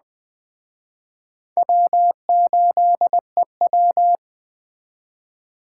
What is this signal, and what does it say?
Morse code, heard as a keyed tone, about 21 wpm, 700 Hz. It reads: W8EW